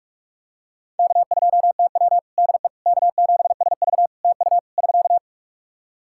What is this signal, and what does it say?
Morse code, heard as a keyed tone, about 41 wpm, 700 Hz. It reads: K2TW DE K7SV TU <SK>
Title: K1USN SST QSO Practice 45wpm